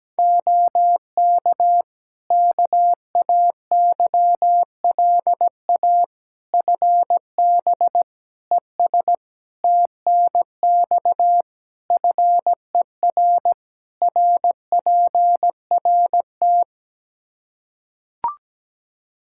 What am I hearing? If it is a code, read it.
Morse code, heard as a keyed tone, about 17 wpm, 700 Hz. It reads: OK KAYLA FB ES TNX FER RPRT
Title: K1USN SST QSO Practice 17wpm